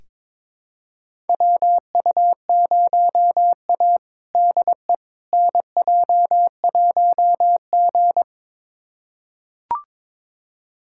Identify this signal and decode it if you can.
Morse code, heard as a keyed tone, about 22 wpm, 700 Hz. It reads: WU0A DE NJ1G
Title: USA Callsign DE Callsign - 4 Characters - 22wpm